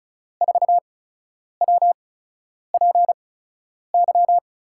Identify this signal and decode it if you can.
Morse code, heard as a keyed tone, about 35 wpm, spaced as for 10 wpm, 700 Hz. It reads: 4WPY
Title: USA Call Signs - 6 Characters - 8x Character Spacing - 35wpm